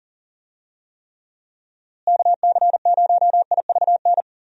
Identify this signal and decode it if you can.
Morse code, heard as a keyed tone, about 40 wpm, 700 Hz. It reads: KC0IVN
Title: USA Call Signs - 6 Characters - Repeated 2x - 40wpm